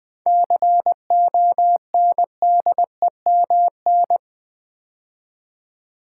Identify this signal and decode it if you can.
Morse code, heard as a keyed tone, about 20 wpm, 700 Hz. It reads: CONDEMN
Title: 7 Letter Words - First 3 Letters Given - 20wpm